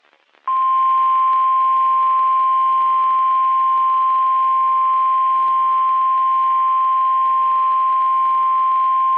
Emergency Alert System (SAME) tones (the 1050 Hz attention tone). It is heard from 0.7 to 8.0 s.
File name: eas messages / EAS WXM31 Elmira